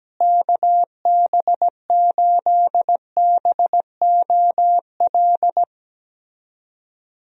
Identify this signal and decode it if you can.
Morse code, heard as a keyed tone, about 17 wpm, 700 Hz. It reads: KB8BOL
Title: USA Call Signs - 6 Characters - Repeated 2x - 17wpm